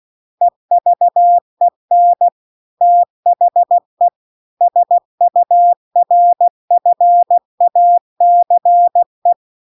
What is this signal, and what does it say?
Morse code, heard as a keyed tone, about 16 wpm, 700 Hz. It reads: EVEN THE SURFACE
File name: CW-The-Big-Twoheaded-River-1-16WPM